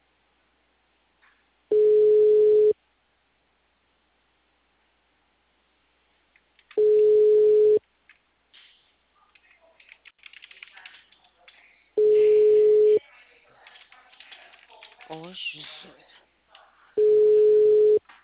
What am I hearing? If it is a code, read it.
Ringback tone (Europe, 425 Hz), with speech.
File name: out- -2016- - - .8389